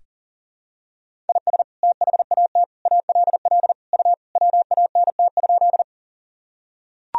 Morse code, heard as a keyed tone, about 37 wpm, 700 Hz. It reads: IS THAT ALL U WANT?
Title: Sentences from Top 200 Words and Abbreviations 40wpm